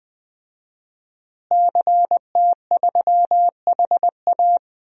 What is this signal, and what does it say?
Morse code, heard as a keyed tone, about 20 wpm, 700 Hz. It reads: CT3HA
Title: DXCC-100 Call Signs - 5 Characters - Repeated 2x - 20wpm